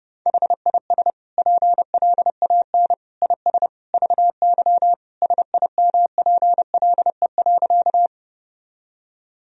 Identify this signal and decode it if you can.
Morse code, heard as a keyed tone, about 32 wpm, 700 Hz. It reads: HIS PLAN IS VY SIMPLE.
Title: Sentences from Top 500 Words and Abbreviations 30wpm